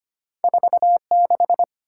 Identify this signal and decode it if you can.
Morse code, heard as a keyed tone, about 25 wpm, 700 Hz. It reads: 46
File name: Two Letter-Number Combinations - 25wpm